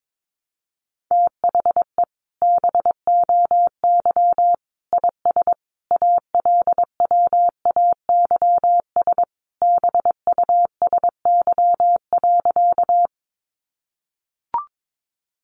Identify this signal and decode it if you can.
Morse code, heard as a keyed tone, about 22 wpm, 700 Hz. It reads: THE BOY IS ALWAYS BUSY.
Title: Additional Easy Phrases 22wpm